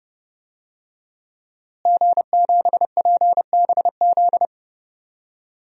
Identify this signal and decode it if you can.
Morse code, heard as a keyed tone, about 30 wpm, 700 Hz. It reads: G7PBZ
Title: DXCC-100 Call Signs - 5 Characters - Repeated 3x - 30wpm